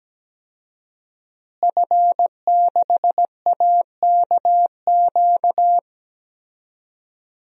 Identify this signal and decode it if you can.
Morse code, heard as a keyed tone, about 17 wpm, 700 Hz. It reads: F6AKQ
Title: DXCC-100 Call Signs - 5 Characters - Repeated 3x - 17wpm